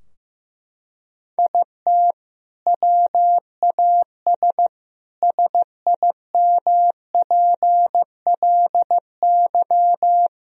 Morse code, heard as a keyed tone, about 15 wpm, 700 Hz. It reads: IT WAS SIMPLY
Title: Sentences from Top 1000 Words - Review Part 4 - 15wpm